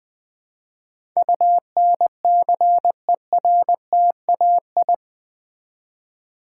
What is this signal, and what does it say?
Morse code, heard as a keyed tone, about 20 wpm, 700 Hz. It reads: UNCERTAI
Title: Build A Word - 4 Letters At A Time - 11-Letter Words - 20wpm